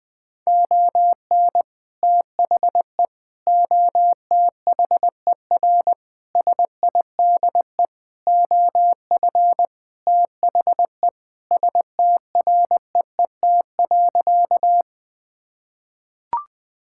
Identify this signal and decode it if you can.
Morse code, heard as a keyed tone, about 20 wpm, 700 Hz. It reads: ON THE OTHER SIDE OF THE STREET.
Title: Sentences with 2-Word Phrases - 20wpm